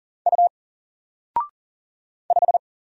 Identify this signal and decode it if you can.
Morse code, heard as a keyed tone, about 39 wpm, 700 Hz. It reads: U 5
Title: Single Letter-Number 40wpm